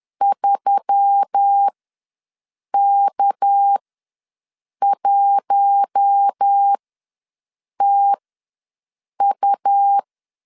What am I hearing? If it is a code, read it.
Morse code, heard as a keyed tone, about 10 wpm, spaced as for 6 wpm, 800 Hz. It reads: 3K1TU
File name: LEZIONI / 6 PKX0 / 6 PK ws 10wpm 3745